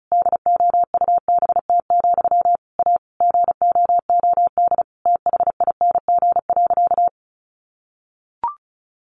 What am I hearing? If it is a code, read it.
Morse code, heard as a keyed tone, about 34 wpm, 700 Hz. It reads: DOUBT, A GOOD THING.
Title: Sentences from Top 1500 Words 35wpm